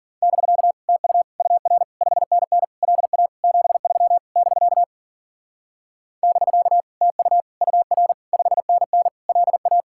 Morse code, heard as a keyed tone, about 50 wpm, 700 Hz. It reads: <BK> TU UR 5NN LA 73 <BK> <BK> TU UR 5NN LA
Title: Speed-Racing Common QSO Phrases - 40wpm